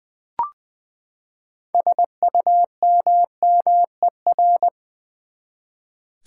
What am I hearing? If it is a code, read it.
Morse code, heard as a keyed tone, about 20 wpm, 700 Hz. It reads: SUMMER
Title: Sentences from Top 1100 Words - New Word Drill Alt v2 - 20wpm